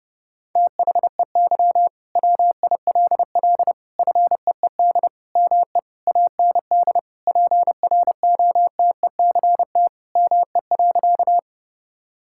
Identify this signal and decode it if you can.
Morse code, heard as a keyed tone, about 30 wpm, 700 Hz. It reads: THEY WILL FEED ME AND PROTECT ME.
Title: Sentences from Top 1600 Words 30wpm